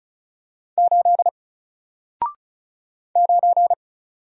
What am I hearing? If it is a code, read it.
Morse code, heard as a keyed tone, about 36 wpm, 700 Hz. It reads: 8 9
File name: Single-Letter-Number-35wpm